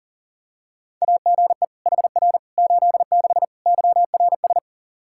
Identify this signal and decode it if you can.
Morse code, heard as a keyed tone, about 37 wpm, 700 Hz. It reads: AGE HR 86 YRS
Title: Practice QSO - Exchange 3 - More Info - 40wpm